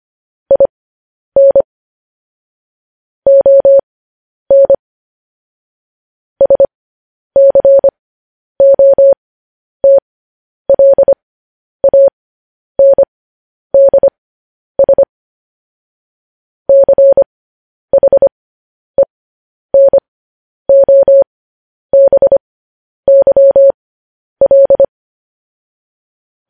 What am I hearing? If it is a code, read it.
Morse code, heard as a keyed tone, about 25 wpm, spaced as for 10 wpm, 550 Hz. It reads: IN ON SCOTLANDS CHENOBYL
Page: news headlines in morse code at 10 wpm